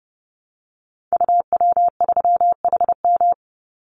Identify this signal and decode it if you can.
Morse code, heard as a keyed tone, about 30 wpm, 700 Hz. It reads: UW3HM
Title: Call Signs Repeated 3x - 30wpm